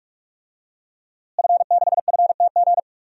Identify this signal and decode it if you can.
Morse code, heard as a keyed tone, about 47 wpm, 700 Hz. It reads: F6FTG
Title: DXCC-100 Call Signs - 5 Characters - Repeated 2x - 45wpm